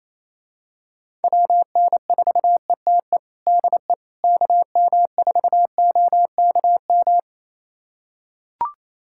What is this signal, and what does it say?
Morse code, heard as a keyed tone, about 28 wpm, 700 Hz. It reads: WN4ETE DE KM4OKM
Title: USA Callsign DE Callsign 28wpm